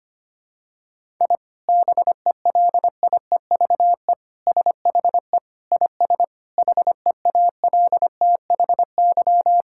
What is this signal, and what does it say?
Morse code, heard as a keyed tone, about 25 wpm, 700 Hz. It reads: I BELIEVE SHE IS HEALTHY
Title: Sentences from Top 1000 Words - Review Part 5 - 25wpm